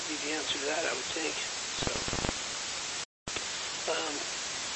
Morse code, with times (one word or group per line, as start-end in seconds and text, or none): none